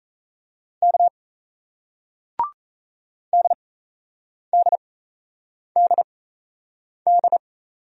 Morse code, heard as a keyed tone, about 38 wpm, spaced as for 8 wpm, 700 Hz. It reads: K DDDD